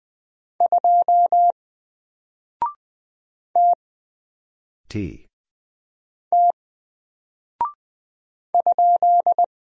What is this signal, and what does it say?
Morse code, heard as a keyed tone, about 20 wpm, 700 Hz. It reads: 2 T T ?
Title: Single Character (Letter:Number:Punctuation:Prosign) Alt 20wpm